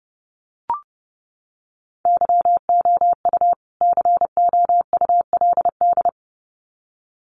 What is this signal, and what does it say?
Morse code, heard as a keyed tone, about 30 wpm, 700 Hz. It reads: YOU COULD
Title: Sets of 2 Words 30wpm